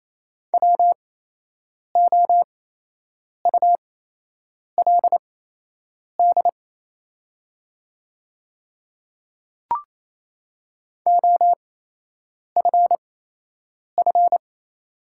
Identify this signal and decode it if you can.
Morse code, heard as a keyed tone, about 27 wpm, spaced as for 8 wpm, 690 Hz. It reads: WOULD OFF
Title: 8x Character Spacing - Top 200 Words - 28wpm